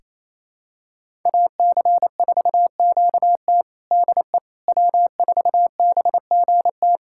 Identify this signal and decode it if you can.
Morse code, heard as a keyed tone, about 28 wpm, 700 Hz. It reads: AC4QT DE W4BGT
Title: USA Callsign DE Callsign - 5 Characters - 28wpm